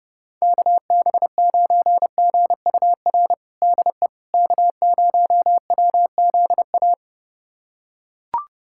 Morse code, heard as a keyed tone, about 30 wpm, 700 Hz. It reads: KB9GUR DE K0WZA
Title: USA Callsign DE Callsign 30wpm